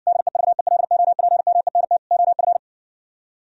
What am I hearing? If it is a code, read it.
Morse code, heard as a keyed tone, about 60 wpm, 700 Hz. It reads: DEVELOPMENT OF